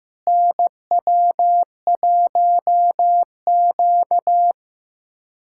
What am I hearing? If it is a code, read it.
Morse code, heard as a keyed tone, about 15 wpm, 700 Hz. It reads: NW1Q
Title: USA Callsign DE Callsign - 4 Characters - 15wpm